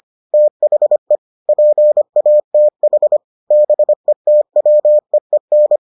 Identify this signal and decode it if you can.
Morse code, heard as a keyed tone, about 25 wpm, 600 Hz. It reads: THE PATH BETWEEN